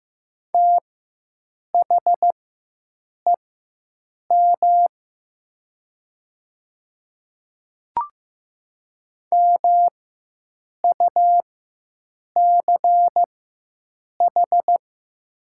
Morse code, heard as a keyed tone, about 15 wpm, spaced as for 7 wpm, 700 Hz. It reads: THEM MUCH
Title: 4x Character Spacing - Top 200 Words - 15wpm